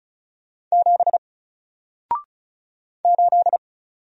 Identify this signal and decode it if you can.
Morse code, heard as a keyed tone, about 36 wpm, 700 Hz. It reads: 7 8
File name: Single-Letter-Number-35wpm